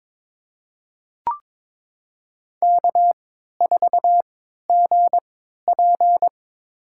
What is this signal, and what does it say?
Morse code, heard as a keyed tone, about 23 wpm, spaced as for 13 wpm, 700 Hz. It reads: K4GP
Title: USA Call Signs - 4 Characters - 3x Character Spacing - 22wpm